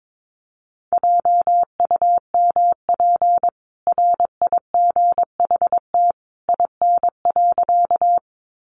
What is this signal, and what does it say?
Morse code, heard as a keyed tone, about 22 wpm, 700 Hz. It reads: JUMP RIGHT IN.
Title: Sentences from Top 1300 Words 22wpm